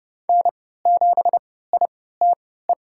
Morse code, heard as a keyed tone, about 31 wpm, spaced as for 17 wpm, 700 Hz. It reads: N7ITE
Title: USA Call Signs - 3x Character Spacing - 30wpm